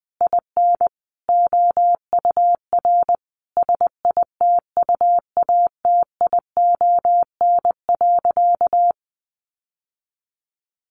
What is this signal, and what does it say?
Morse code, heard as a keyed tone, about 20 wpm, 700 Hz. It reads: IN OUR SITUATION.
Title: Sentences from Top 1200 Words 20wpm